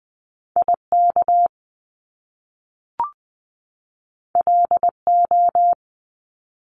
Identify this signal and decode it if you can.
Morse code, heard as a keyed tone, about 20 wpm, 700 Hz. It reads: IK LO
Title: Top Two Letter Combinations in Words 20wpm